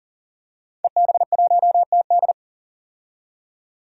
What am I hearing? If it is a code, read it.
Morse code, heard as a keyed tone, about 37 wpm, 700 Hz. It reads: EB1TD
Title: DXCC-100 Call Signs - 5 Characters - 40wpm